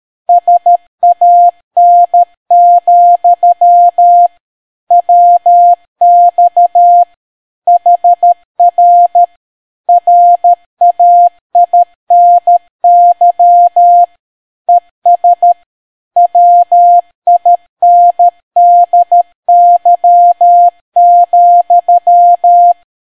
Morse code, heard as a keyed tone, about 13 wpm, 700 Hz. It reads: SAN, WX HR RAINY ES WINDY,